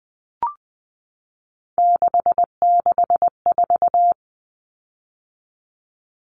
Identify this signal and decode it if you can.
Morse code, heard as a keyed tone, about 20 wpm, 700 Hz. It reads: 664
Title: Common Cut Numbers - 3-Digit 20wpm